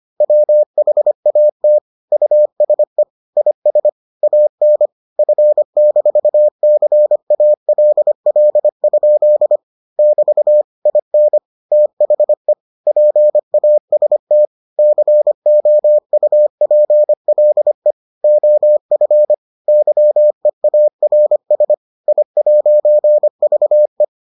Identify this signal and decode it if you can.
Morse code, heard as a keyed tone, about 25 wpm, 600 Hz. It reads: WHAT USE IS AN F-CALL? <BT> IN THE PAST COUPLE OF YEARS I'VE